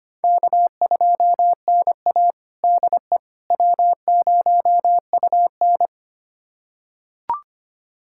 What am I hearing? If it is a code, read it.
Morse code, heard as a keyed tone, about 25 wpm, 700 Hz. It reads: K2NA DE W0UN
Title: USA Callsign DE Callsign - 4 Characters - 25wpm